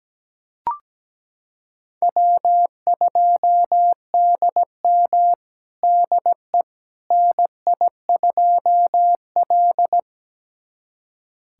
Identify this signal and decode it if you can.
Morse code, heard as a keyed tone, about 17 wpm, 700 Hz. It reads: W2DM DE NI2L
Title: USA Callsign DE Callsign - 4 Characters - 17wpm